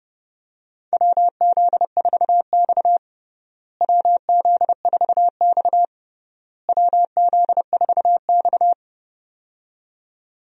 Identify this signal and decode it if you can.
Morse code, heard as a keyed tone, about 30 wpm, 700 Hz. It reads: WZ4X WZ4X WZ4X
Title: USA Call Signs - 4 Characters - Repeated 3x - 30wpm